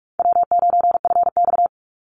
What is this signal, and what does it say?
Morse code, heard as a keyed tone, about 44 wpm, 700 Hz. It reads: W9FX
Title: Easy Call Signs Repeated 2x - 45wpm